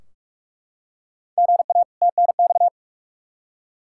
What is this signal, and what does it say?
Morse code, heard as a keyed tone, about 44 wpm, 700 Hz. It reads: GA TNX
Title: K1USN SST QSO Practice 45wpm